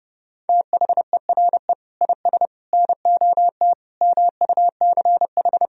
Morse code, heard as a keyed tone, about 32 wpm, 700 Hz. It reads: THERE IS NOT MUCH
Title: Sentences from Top 1000 Words - Review Part 4 - 30wpm